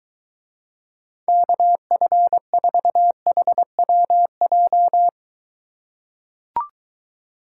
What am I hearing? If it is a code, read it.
Morse code, heard as a keyed tone, about 23 wpm, 700 Hz. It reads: KF4HWJ